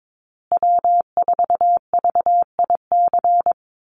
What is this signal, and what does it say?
Morse code, heard as a keyed tone, about 22 wpm, 700 Hz. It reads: W4VIC
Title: Call Signs Repeated 2x - 22wpm